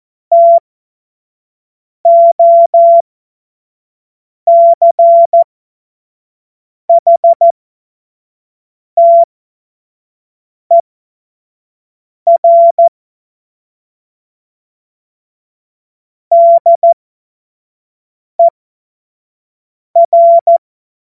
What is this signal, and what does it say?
Morse code, heard as a keyed tone, about 14 wpm, spaced as for 5 wpm, 680 Hz. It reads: TOCHTER DER
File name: Morsekurs / Witze / Witz 06. September 2022 25 F70